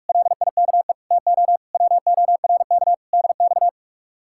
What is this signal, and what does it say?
Morse code, heard as a keyed tone, about 56 wpm, 700 Hz. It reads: LIKE TO WORK DX
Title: Speed-Racing Common QSO Phrases - 45wpm